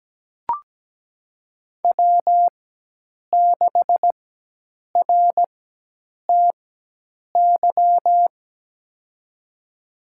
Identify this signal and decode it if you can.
Morse code, heard as a keyed tone, about 17 wpm, spaced as for 8 wpm, 700 Hz. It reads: W6RTY
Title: USA Call Signs - 5 Characters - 4x Character Spacing - 17wpm